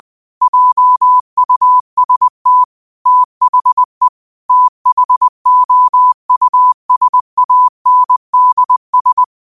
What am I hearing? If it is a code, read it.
Morse code, heard as a keyed tone, about 20 wpm, 1000 Hz. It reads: JUST THE THOUSANDS
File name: qotdfiles / 20wpm